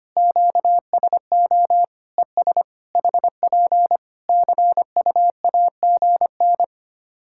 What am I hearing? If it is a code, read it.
Morse code, heard as a keyed tone, about 25 wpm, 700 Hz. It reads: QSO ES HP CUAGN
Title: Practice QSO - Exchange 4 - Ending - 25wpm